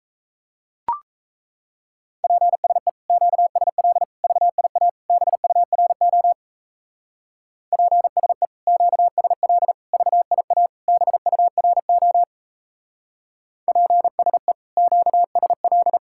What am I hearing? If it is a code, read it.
Morse code, heard as a keyed tone, about 35 wpm, 700 Hz. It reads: PSE QSL VIA BURO PSE QSL VIA BURO PSE QSL
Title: Speed-Racing Common QSO Phrases - 28wpm